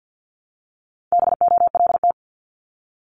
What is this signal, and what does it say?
Morse code, heard as a keyed tone, about 50 wpm, 700 Hz. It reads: BOLT
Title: Multi-letter Phonemes Repeated 3x 50wpm